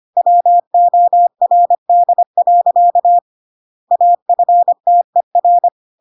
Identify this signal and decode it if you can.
Morse code, heard as a keyed tone, about 25 wpm, 700 Hz. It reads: WORD. AFTER